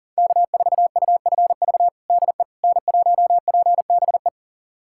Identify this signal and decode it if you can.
Morse code, heard as a keyed tone, about 40 wpm, 700 Hz. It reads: K4UFV DE N1PBE
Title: USA Callsign DE Callsign - 5 Characters - 40wpm